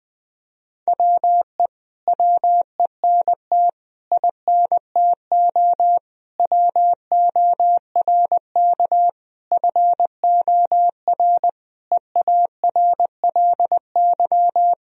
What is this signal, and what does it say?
Morse code, heard as a keyed tone, about 20 wpm, 700 Hz. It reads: WE WENT INTO WORK FOR EARLY
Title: Sentences from Top 1000 Words - Review Part 4 - 20wpm